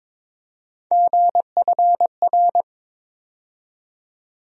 Morse code, heard as a keyed tone, about 22 wpm, 700 Hz. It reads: GFR
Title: USA Call Sign Postfixes - 3 Letters 22wpm